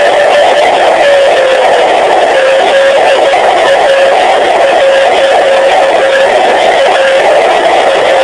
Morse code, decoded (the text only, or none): none